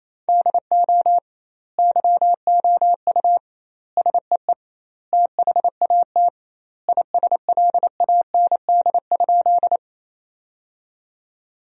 Morse code, heard as a keyed tone, about 28 wpm, 700 Hz. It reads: DO YOU SEE THAT ISLAND?
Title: 2x Word-Spacing - Sentences from Top 500 Words - 28wpm